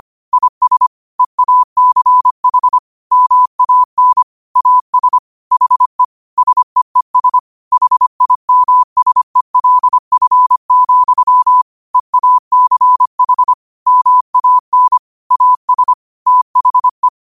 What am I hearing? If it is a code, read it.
Morse code, heard as a keyed tone, about 25 wpm, 1000 Hz. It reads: IS EACH MAN AS HE SEES HIMSELF, EACH MAN AS THE
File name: qotdfiles / 25wpm